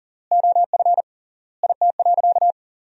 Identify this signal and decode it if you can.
Morse code, heard as a keyed tone, about 40 wpm, 700 Hz. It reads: OF IT.
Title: 3x Word-Spacing - Sentences from Top 200 Words - 40wpm